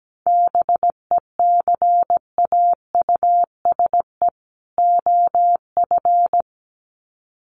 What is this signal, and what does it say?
Morse code, heard as a keyed tone, about 17 wpm, 700 Hz. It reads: BECAUSE OF
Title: Sets of 2 Words 17wpm